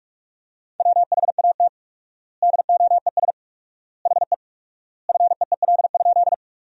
Morse code, heard as a keyed tone, about 46 wpm, 700 Hz. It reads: WHAT DOES HE FEEL?
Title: 4x Word-Spacing - Sentences from Top 400 Words - 45wpm